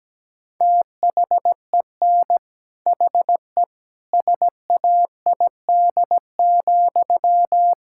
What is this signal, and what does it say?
Morse code, heard as a keyed tone, about 17 wpm, 700 Hz. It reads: THEN HE SAID,
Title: Sentences from Top 1600 Words 17wpm